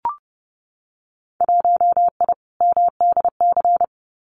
Morse code, heard as a keyed tone, about 30 wpm, 700 Hz. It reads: 1I MDC
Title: Winter Field Day Exchanges 30wpm